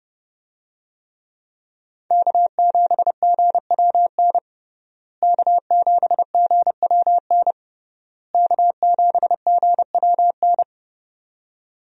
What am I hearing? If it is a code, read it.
Morse code, heard as a keyed tone, about 30 wpm, 700 Hz. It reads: K7GWN K7GWN K7GWN
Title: USA Call Signs - 5 Characters - Repeated 3x - 30wpm